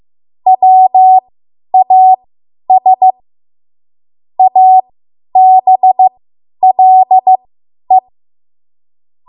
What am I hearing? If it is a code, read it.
Morse code, heard as a keyed tone, about 15 wpm, spaced as for 10 wpm, 750 Hz. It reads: WAS ABLE
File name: Morse / 10WPM